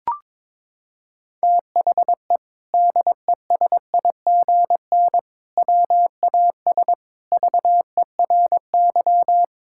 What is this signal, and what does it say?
Morse code, heard as a keyed tone, about 22 wpm, 700 Hz. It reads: THE DESIGN WAS VERY
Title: Sentences from Top 1000 Words - Review Part 5 - 22wpm